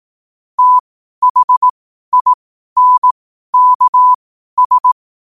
Morse code, heard as a keyed tone, about 18 wpm, spaced as for 12 wpm, 1000 Hz. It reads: THINKS
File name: qotdfiles / 12wpm